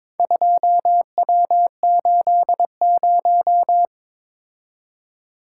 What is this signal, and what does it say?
Morse code, heard as a keyed tone, about 21 wpm, 700 Hz. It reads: 2W80
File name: practice / 22wpm / Four Letter-Number Combinations - 22wpm